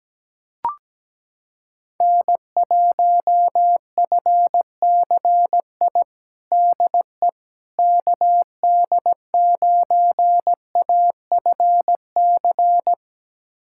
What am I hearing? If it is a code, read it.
Morse code, heard as a keyed tone, about 17 wpm, 700 Hz. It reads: N1FCI DE KD9AFC
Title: USA Callsign DE Callsign 17wpm